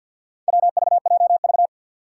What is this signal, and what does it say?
Morse code, heard as a keyed tone, about 50 wpm, 700 Hz. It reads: WVJV
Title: Four Challenging Letter-Number Combinations 50wpm